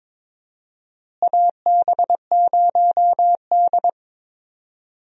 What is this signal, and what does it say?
Morse code, heard as a keyed tone, about 22 wpm, 700 Hz. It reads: AB0D
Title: USA Call Signs - 4 Characters - Repeated 2x - 22wpm